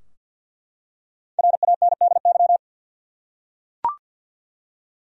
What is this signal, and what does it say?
Morse code, heard as a keyed tone, about 48 wpm, 700 Hz. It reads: RANDY
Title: USA Common Names - Max 5 Letters - 50wpm